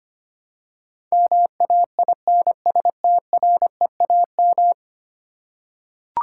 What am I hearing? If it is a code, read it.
Morse code, heard as a keyed tone, about 25 wpm, 700 Hz. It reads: MAINSTREAM
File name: Ten Letter Words - 25wpm